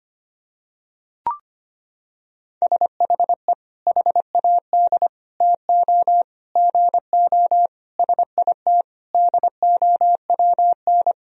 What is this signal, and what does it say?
Morse code, heard as a keyed tone, about 25 wpm, 700 Hz. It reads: SHE HAD TO GO SIT DOWN
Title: Sentences from Top 1000 Words - Review Part 2 - 25wpm